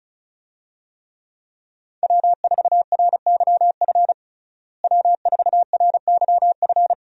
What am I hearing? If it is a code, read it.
Morse code, heard as a keyed tone, about 36 wpm, 700 Hz. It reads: W4RYF W4RYF
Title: USA Call Signs - 5 Characters - Repeated 2x - 35wpm